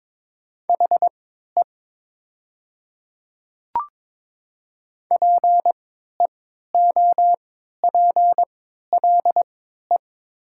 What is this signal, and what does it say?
Morse code, heard as a keyed tone, about 22 wpm, spaced as for 12 wpm, 700 Hz. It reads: HE PEOPLE
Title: 3x Character Spacing - Top 100 Words - 22wpm